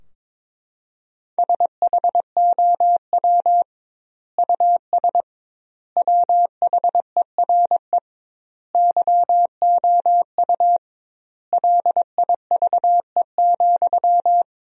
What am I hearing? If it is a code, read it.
Morse code, heard as a keyed tone, about 22 wpm, 700 Hz. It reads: SHOW US WHERE YOU LIVE,
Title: 2x Word-Spacing - Sentences from Top 200 Words - 22wpm